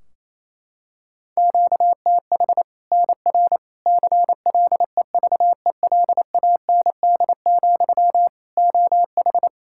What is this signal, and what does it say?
Morse code, heard as a keyed tone, about 27 wpm, 700 Hz. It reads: QTH NR CLEVELAND, OH
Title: K1USN SST QSO Practice 28wpm